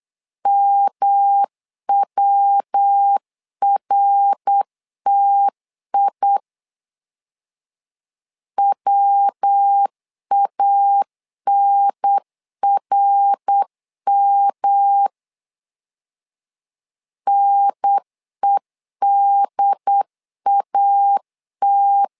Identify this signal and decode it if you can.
Morse code, heard as a keyed tone, about 8 wpm, 800 Hz. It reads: MWRTI WANRM NEDAT
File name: LEZIONI / 3 MWRO / 3 MWR 8wpm 2444